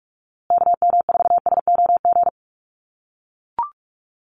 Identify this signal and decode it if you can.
Morse code, heard as a keyed tone, about 44 wpm, 700 Hz. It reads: KM4SOG
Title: USA Call Signs Repeated 3x 45wpm